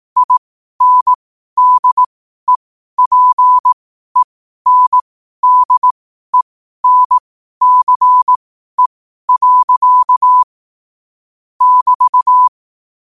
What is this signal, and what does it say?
Morse code, heard as a keyed tone, about 18 wpm, spaced as for 12 wpm, 1000 Hz. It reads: INDEPENDENCE. <BT>
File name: qotdfiles / 12wpm